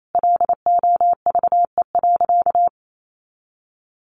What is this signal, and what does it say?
Morse code, heard as a keyed tone, about 28 wpm, 700 Hz. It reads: LOVE.
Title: Sentences from Top 400 Words 28wpm